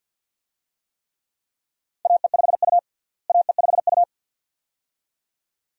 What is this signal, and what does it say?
Morse code, heard as a keyed tone, about 52 wpm, 700 Hz. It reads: AE5U AE5U
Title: USA Call Signs - 4 Characters - Repeated 2x - 50wpm